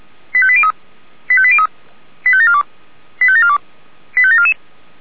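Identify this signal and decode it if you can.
Five-tone selective call (ZVEI), heard as quick runs of five tones, tones of about 70 ms. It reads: 85803 85803 85742 85742 85744
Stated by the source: ZVEI five-tone selective call received by SDR.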